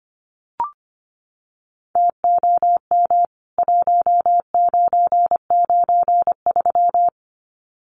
Morse code, heard as a keyed tone, about 25 wpm, 700 Hz. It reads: TOM 1993
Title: CWT Exchange 25wpm